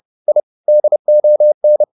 Morse code, heard as a keyed tone, about 30 wpm, 600 Hz. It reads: I DON
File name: the old man and the sea / the old man and the sea 1 30wpm